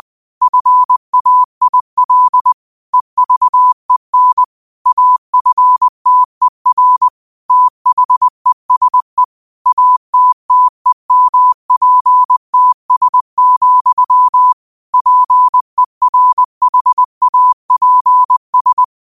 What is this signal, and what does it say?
Morse code, heard as a keyed tone, about 20 wpm, 1000 Hz. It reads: FAIL EVEN AFTER THESE ATTEMPTS, PERHAPS
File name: qotdfiles / 20wpm